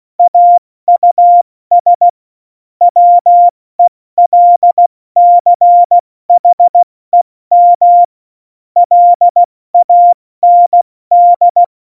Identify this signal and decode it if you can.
Morse code, heard as a keyed tone, about 16 wpm, 700 Hz. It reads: AUS WELCHEM LAND
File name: Morsekurs / Witze / Witz 07. Mai 2019, 70 F80